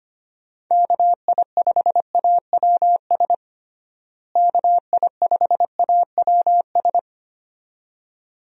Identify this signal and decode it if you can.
Morse code, heard as a keyed tone, about 25 wpm, 700 Hz. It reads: KI5AWS KI5AWS
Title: USA Call Signs - 6 Characters - Repeated 2x - 25wpm